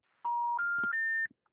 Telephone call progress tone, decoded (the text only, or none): special information tone